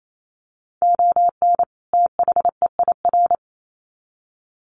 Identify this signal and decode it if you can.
Morse code, heard as a keyed tone, about 28 wpm, 700 Hz. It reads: ON THEIR
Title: Sets of 2 Words 28wpm